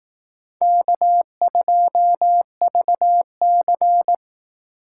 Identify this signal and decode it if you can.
Morse code, heard as a keyed tone, about 18 wpm, 700 Hz. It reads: K2VC
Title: Speed Racing - USA 1x2 Call Signs 15wpm